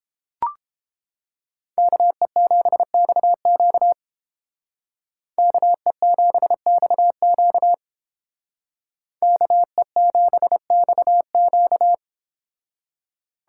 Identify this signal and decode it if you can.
Morse code, heard as a keyed tone, about 30 wpm, 700 Hz. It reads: KE7XQ KE7XQ KE7XQ